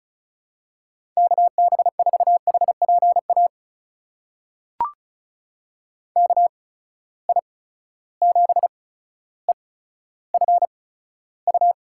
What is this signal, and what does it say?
Morse code, heard as a keyed tone, about 36 wpm, spaced as for 15 wpm, 700 Hz. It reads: KB4HPA KI7EFU
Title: USA Call Signs - 6 Characters - 8x Character Spacing - 35wpm